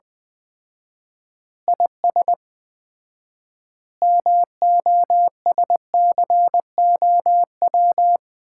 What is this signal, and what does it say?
Morse code, heard as a keyed tone, about 21 wpm, 700 Hz. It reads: IS MOSCOW
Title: Common QSO Phrases - 4x Word-Spacing - 20wpm